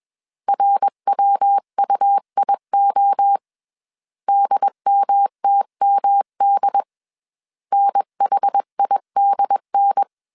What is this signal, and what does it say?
Morse code, heard as a keyed tone, about 21 wpm, 800 Hz. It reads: RWUIO DMTMD NHIDN
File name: CW / DET / 1 3 1500C S B 20wpm 1754